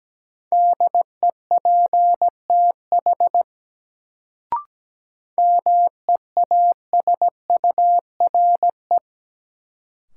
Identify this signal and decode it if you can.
Morse code, heard as a keyed tone, about 17 wpm, 700 Hz. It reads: DEPTH MEASURE
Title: Sentences from Top 1200 Words - New Word Drill Alt - 17wpm